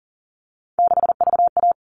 Morse code, heard as a keyed tone, about 40 wpm, 700 Hz. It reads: 6VA
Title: All Cut Numbers - 3-Digit 40wpm